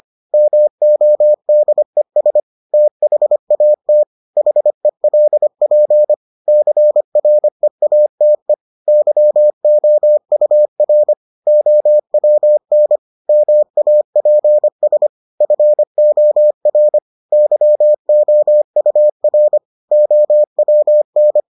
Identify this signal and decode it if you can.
Morse code, heard as a keyed tone, about 25 wpm, 600 Hz. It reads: MODES THAT HELP CREATE YOUR OWN MAPS FOR YOUR OWN